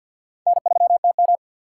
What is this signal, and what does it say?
Morse code, heard as a keyed tone, about 52 wpm, 700 Hz. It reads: N3TM
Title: USA Call Signs - 4x Character Spacing - 50wpm